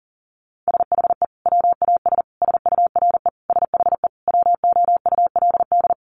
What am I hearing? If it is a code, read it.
Morse code, heard as a keyed tone, about 40 wpm, 700 Hz. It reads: SHE WAS SURE SHE WOULD
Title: Sentences from Top 1500 Words 40wpm